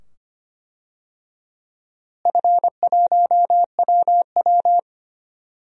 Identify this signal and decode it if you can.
Morse code, heard as a keyed tone, about 25 wpm, 700 Hz. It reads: F1WW
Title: DXCC-100 Call Signs - 4 Characters - Repeated 2x - 25wpm